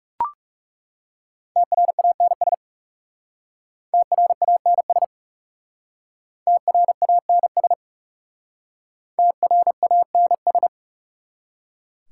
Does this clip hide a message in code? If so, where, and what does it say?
Yes: Morse code from 1.6 to 10.7 s, heard as a keyed tone, about 40 wpm, 700 Hz. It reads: TRANS TRANS TRANS TRANS